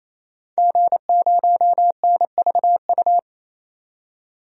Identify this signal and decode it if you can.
Morse code, heard as a keyed tone, about 27 wpm, 700 Hz. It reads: G0NVU
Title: DXCC-100 Call Signs - 5 Characters - 28wpm